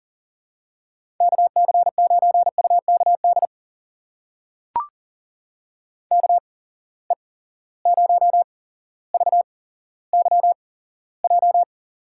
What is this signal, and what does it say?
Morse code, heard as a keyed tone, about 39 wpm, spaced as for 17 wpm, 700 Hz. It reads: KC9UKD KE0VYJ
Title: USA Call Signs - 6 Characters - 8x Character Spacing - 40wpm